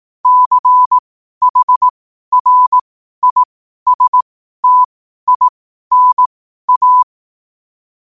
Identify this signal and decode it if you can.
Morse code, heard as a keyed tone, about 18 wpm, spaced as for 12 wpm, 1000 Hz. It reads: CHRISTINA
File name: qotdfiles / 12wpm